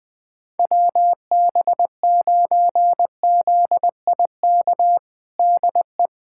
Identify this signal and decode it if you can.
Morse code, heard as a keyed tone, about 20 wpm, 700 Hz. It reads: WB9ZIK DE
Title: USA Callsign DE Callsign - 6 Characters - 20wpm